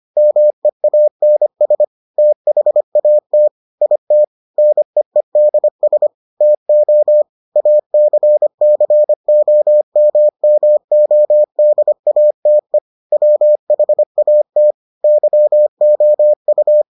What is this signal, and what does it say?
Morse code, heard as a keyed tone, about 25 wpm, 600 Hz. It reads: MEANS THAT IT NEEDS TO ACCOMMODATE WHAT YOU